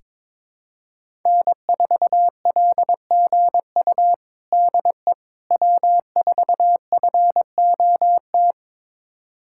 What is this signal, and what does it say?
Morse code, heard as a keyed tone, about 22 wpm, 700 Hz. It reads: N4LGU DE W4FOT
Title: USA Callsign DE Callsign - 5 Characters - 22wpm